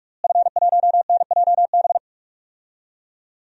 Morse code, heard as a keyed tone, about 41 wpm, 700 Hz. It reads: F1NJB
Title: DXCC-100 Call Signs - 5 Characters - Repeated 2x - 45wpm